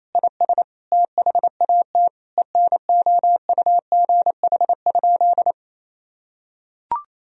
Morse code, heard as a keyed tone, about 28 wpm, 700 Hz. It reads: IS THAT ENOUGH?
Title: Sentences from Top 1000 Words - Review Part 2 - 28wpm